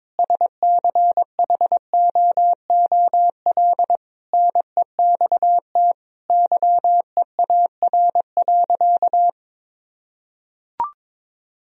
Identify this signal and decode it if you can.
Morse code, heard as a keyed tone, about 22 wpm, 700 Hz. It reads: SCHOOL NEXT YEAR.
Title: Sentences from Top 1000 Words - Review Part 3 - 22wpm